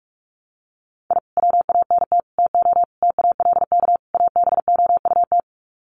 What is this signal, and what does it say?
Morse code, heard as a keyed tone, about 44 wpm, 700 Hz. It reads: I WANT TO TALK ABOUT
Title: Sets of 5 Words Encore 45wpm